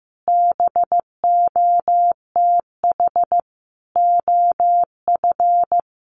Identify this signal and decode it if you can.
Morse code, heard as a keyed tone, about 15 wpm, 700 Hz. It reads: BOTH OF
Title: Sentences from Top 1200 Words 15wpm